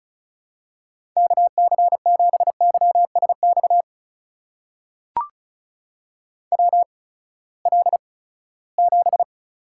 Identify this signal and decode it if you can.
Morse code, heard as a keyed tone, about 36 wpm, spaced as for 20 wpm, 700 Hz. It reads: KC7YSX WL7
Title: USA Call Signs - 6 Characters - 8x Character Spacing - 35wpm